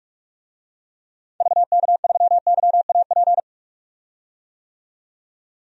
Morse code, heard as a keyed tone, about 43 wpm, 700 Hz. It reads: VK3YAP
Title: DXCC-100 Call Signs - 6 Characters - 45wpm